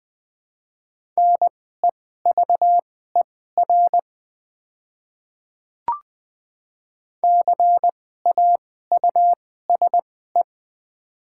Morse code, heard as a keyed tone, about 20 wpm, spaced as for 14 wpm, 700 Hz. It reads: NEVER CAUSE
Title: 2x Character Spacing - Top 300 Words - 20wpm